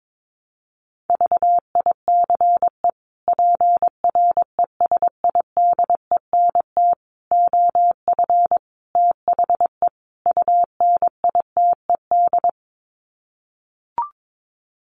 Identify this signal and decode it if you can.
Morse code, heard as a keyed tone, about 22 wpm, 700 Hz. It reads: VICE PRESIDENT OF THE UNITED
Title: Sets of 5 Words Encore2 22wpm